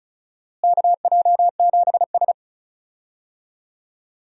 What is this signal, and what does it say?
Morse code, heard as a keyed tone, about 35 wpm, 700 Hz. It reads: KJ7S
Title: USA Call Signs - 4 Characters - Repeated 2x - 35wpm